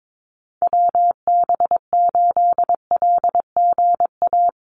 Morse code, heard as a keyed tone, about 22 wpm, 700 Hz. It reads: WB8LGA
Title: USA Call Signs Repeated 2x 22wpm